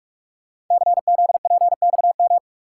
Morse code, heard as a keyed tone, about 45 wpm, 690 Hz. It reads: CZPXM
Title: Code Groups - 5 Letters - 45wpm